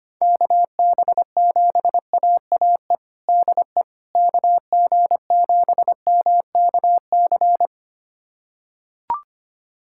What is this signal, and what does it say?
Morse code, heard as a keyed tone, about 25 wpm, 700 Hz. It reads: KB7AAE DE KG7MKC
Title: USA Callsign DE Callsign - 6 Characters - 25wpm